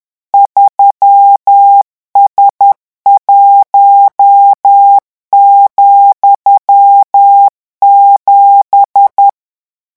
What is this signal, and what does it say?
Morse code, heard as a keyed tone, about 10 wpm, 800 Hz. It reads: 3S1,7
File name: morse mp3 / revision13-vitesse 10 mots